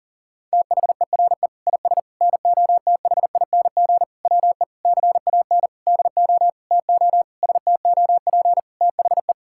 Morse code, heard as a keyed tone, about 37 wpm, 700 Hz. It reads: THERE IS NOTHING WE CAN DO TO STOP THE
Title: Sentences from Top 1000 Words - Review Part 6 - 40wpm